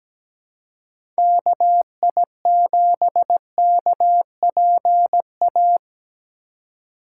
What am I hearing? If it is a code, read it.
Morse code, heard as a keyed tone, about 17 wpm, 700 Hz. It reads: KI7KPA
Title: USA Call Signs - 6 Characters - 3x Character Spacing - 17wpm